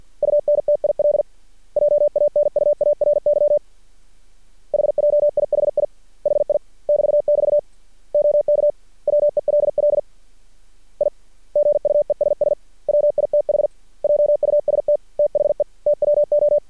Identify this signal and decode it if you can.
Morse code, heard as a keyed tone, about 53 wpm, 590 Hz. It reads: UNTIL JANUARY HJIHI HI <BT><BT> OK WELL I GUESS WITH JUST THE TWO
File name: archived / CRSnet